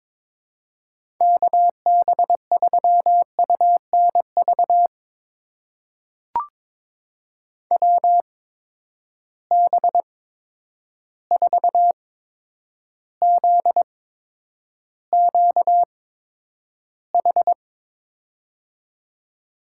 Morse code, heard as a keyed tone, about 22 wpm, spaced as for 9 wpm, 700 Hz. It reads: KB3UNV WB4ZQH
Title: USA Call Signs - 6 Characters - 8x Character Spacing - 22wpm